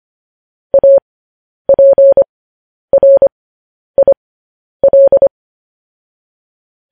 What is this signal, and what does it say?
Morse code, heard as a keyed tone, about 25 wpm, spaced as for 10 wpm, 550 Hz. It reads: APRIL
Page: news-headlines-in-morse-code-at-10-wpm